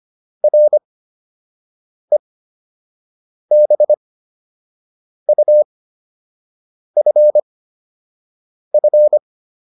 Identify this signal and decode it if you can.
Morse code, heard as a keyed tone, about 25 wpm, spaced as for 6 wpm, 600 Hz. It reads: REBUFF